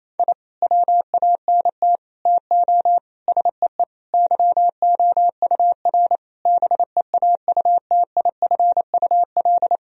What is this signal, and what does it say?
Morse code, heard as a keyed tone, about 28 wpm, 700 Hz. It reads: I WANT TO SEE YOUR BEAUTIFUL
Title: Sentences from Top 1000 Words - Review Part 1 - 28wpm